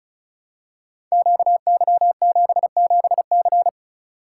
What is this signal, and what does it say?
Morse code, heard as a keyed tone, about 36 wpm, 700 Hz. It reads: QY77C
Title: Code Groups - 5 Letter:Numbers - 35wpm